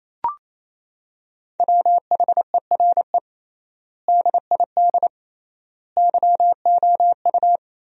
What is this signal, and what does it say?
Morse code, heard as a keyed tone, about 28 wpm, 700 Hz. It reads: WHERE DID YOU
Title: 3x Word-Spacing - Sentences from Top 200 Words - 28wpm